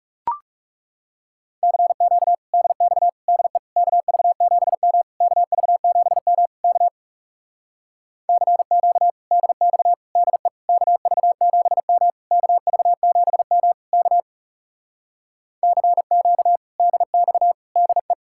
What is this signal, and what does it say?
Morse code, heard as a keyed tone, about 40 wpm, 700 Hz. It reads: CQ DX DE KV7M KV7M K CQ DX DE KV7M KV7M K CQ DX DE